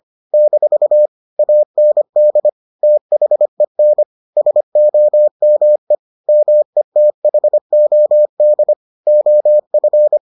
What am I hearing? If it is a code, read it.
Morse code, heard as a keyed tone, about 25 wpm, 600 Hz. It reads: - AND THEN SOME METHOD OF